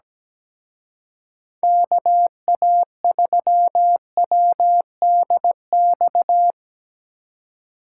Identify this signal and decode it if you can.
Morse code, heard as a keyed tone, about 19 wpm, 700 Hz. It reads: KA3WDX
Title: Speed Racing - USA 2x3 Call Signs 17wpm